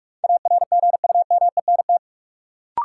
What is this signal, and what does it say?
Morse code, heard as a keyed tone, about 46 wpm, 700 Hz. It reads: ARGUMENT
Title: Sentences from Top 1400 Words - New Word Drill Alt - 45wpm